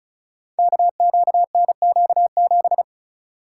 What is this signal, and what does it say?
Morse code, heard as a keyed tone, about 34 wpm, 700 Hz. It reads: KQNQ7
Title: Code Groups - 5 Letter:Numbers - 35wpm